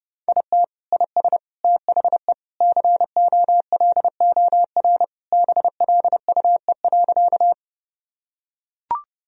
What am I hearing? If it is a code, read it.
Morse code, heard as a keyed tone, about 31 wpm, 700 Hz. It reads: IT IS THE COLOR BLUE.
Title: Sentences from Top 1000 Words - Review Part 3 - 30wpm